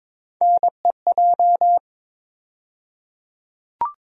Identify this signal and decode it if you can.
Morse code, heard as a keyed tone, about 22 wpm, 700 Hz. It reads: NEJ
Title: USA Call Sign Postfixes - 3 Letters 22wpm